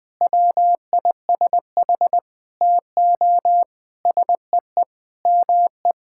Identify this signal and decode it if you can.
Morse code, heard as a keyed tone, about 20 wpm, 700 Hz. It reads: WISH TO SEE ME
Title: Sentences from Top 1600 Words 20wpm